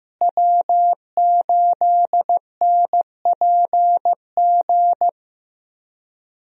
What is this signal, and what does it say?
Morse code, heard as a keyed tone, about 15 wpm, 700 Hz. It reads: W8NPG
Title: USA Call Signs - 5 Characters - Repeated 3x - 15wpm